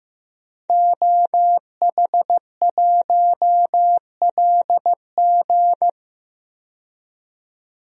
Morse code, heard as a keyed tone, about 15 wpm, 700 Hz. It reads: OH1LG
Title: DXCC-100 Call Signs - 5 Characters - 15wpm